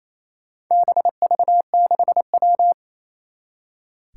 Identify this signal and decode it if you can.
Morse code, heard as a keyed tone, about 27 wpm, 700 Hz. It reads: BV6W
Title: Four Challenging Letter-Number Combinations 28wpm